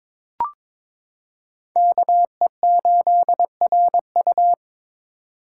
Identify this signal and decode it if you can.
Morse code, heard as a keyed tone, about 21 wpm, 700 Hz. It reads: KE8RU
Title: K1USN SST QSO Practice - 22wpm